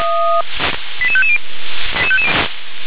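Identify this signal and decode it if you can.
Five-tone selective call (ZVEI), heard as quick runs of five tones, tones of about 70 ms. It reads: 88550
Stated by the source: ZVEI five-tone call received on SDR.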